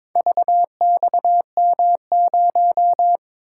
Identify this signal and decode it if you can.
Morse code, heard as a keyed tone, about 22 wpm, 700 Hz. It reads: VXM0
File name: practice / 22wpm / Four Letter-Number Combinations - 22wpm